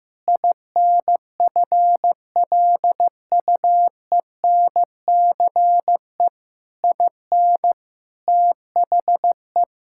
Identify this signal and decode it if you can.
Morse code, heard as a keyed tone, about 15 wpm, 700 Hz. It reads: INFLUENCE IN THE
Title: Sentences from Top 1600 Words 15wpm